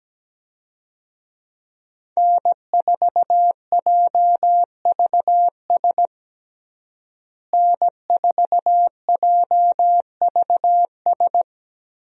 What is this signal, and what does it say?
Morse code, heard as a keyed tone, about 17 wpm, 700 Hz. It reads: N4JVS N4JVS
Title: USA Call Signs - 5 Characters - Repeated 3x - 17wpm